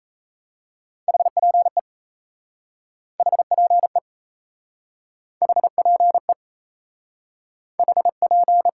Morse code, heard as a keyed tone, about 37 wpm, 700 Hz. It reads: HPE HPE HPE HP